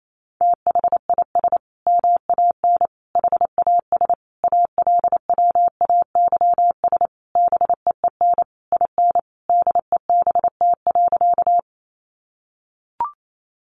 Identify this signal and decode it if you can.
Morse code, heard as a keyed tone, about 28 wpm, 700 Hz. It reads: THIS MAN HAS ALWAYS BEEN IN DEBT.
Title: Sentences from Top 1300 Words 28wpm